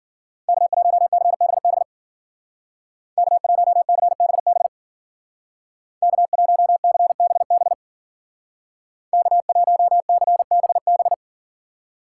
Morse code, heard as a keyed tone, about 50 wpm, 700 Hz. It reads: K1CBB K1CBB K1CBB K1CBB